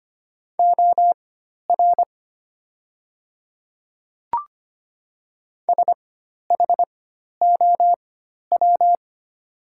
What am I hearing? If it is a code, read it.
Morse code, heard as a keyed tone, about 25 wpm, spaced as for 12 wpm, 700 Hz. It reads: OR SHOW
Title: 4x Character Spacing - Top 300 Words - 25wpm